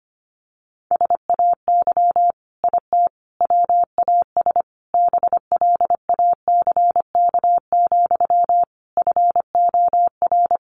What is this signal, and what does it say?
Morse code, heard as a keyed tone, about 25 wpm, 700 Hz. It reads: SAY IT WAS BLACK, FOR
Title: Sentences from Top 800 Words 25wpm